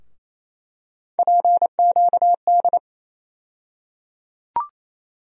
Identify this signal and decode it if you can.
Morse code, heard as a keyed tone, about 28 wpm, 700 Hz. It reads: PQD
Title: USA Call Sign Postfixes - 3 Letters 28wpm